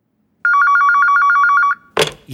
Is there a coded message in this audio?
No.